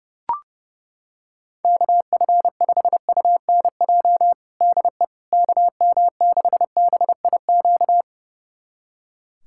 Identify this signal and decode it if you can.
Morse code, heard as a keyed tone, about 30 wpm, 700 Hz. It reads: KF5UNJ DE KM6BIQ
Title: USA Callsign DE Callsign 30wpm